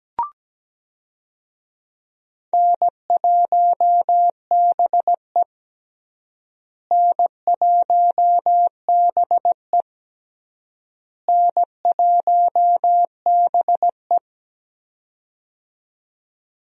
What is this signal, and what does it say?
Morse code, heard as a keyed tone, about 17 wpm, 700 Hz. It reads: N1BE N1BE N1BE
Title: USA Call Signs - 4 Characters - Repeated 3x - 17wpm